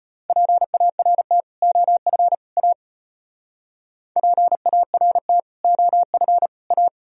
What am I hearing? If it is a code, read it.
Morse code, heard as a keyed tone, about 31 wpm, 700 Hz. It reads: PART OF A PART OF A
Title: Speed-Racing - Sets of 3 Words 25wpm